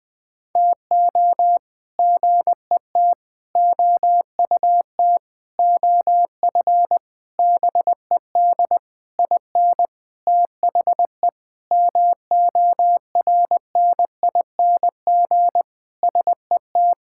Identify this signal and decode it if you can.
Morse code, heard as a keyed tone, about 20 wpm, 700 Hz. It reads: TO GET OUT OF BED IN THE MORNING SET
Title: Sentences with 8-Word Phrases - 20wpm